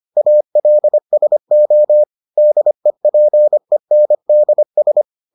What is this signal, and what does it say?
Morse code, heard as a keyed tone, about 25 wpm, 600 Hz. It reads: ALSO DEPENDS